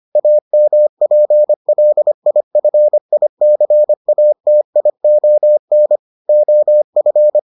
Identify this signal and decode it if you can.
Morse code, heard as a keyed tone, about 25 wpm, 600 Hz. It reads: AMPLIFICATION OF